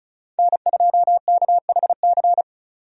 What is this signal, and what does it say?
Morse code, heard as a keyed tone, about 36 wpm, 700 Hz. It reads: N2KHC
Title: USA Call Signs - 5 Characters - 35wpm